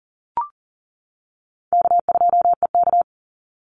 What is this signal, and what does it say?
Morse code, heard as a keyed tone, about 40 wpm, 700 Hz. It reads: K2EK
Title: Easy Call Signs - 40wpm